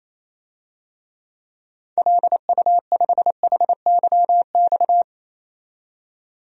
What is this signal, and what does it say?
Morse code, heard as a keyed tone, about 28 wpm, 700 Hz. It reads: LU5HYX
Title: DXCC-100 Call Signs - 6 Characters - 28wpm